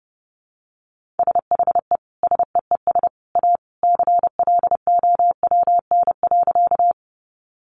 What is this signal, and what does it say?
Morse code, heard as a keyed tone, about 30 wpm, 700 Hz. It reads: SHE SEES A CLOWN.
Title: Easy Phrases 30wpm